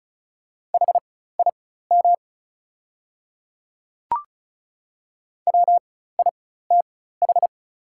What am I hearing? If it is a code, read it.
Morse code, heard as a keyed tone, about 35 wpm, spaced as for 16 wpm, 690 Hz. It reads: HIM WITH
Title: 4x Character Spacing - Top 100 Words - 35wpm